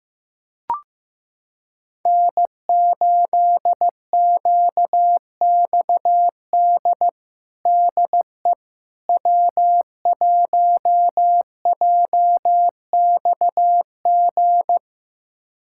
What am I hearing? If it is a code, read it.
Morse code, heard as a keyed tone, about 15 wpm, 700 Hz. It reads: N8QXD DE W1JXG
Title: USA Callsign DE Callsign 15wpm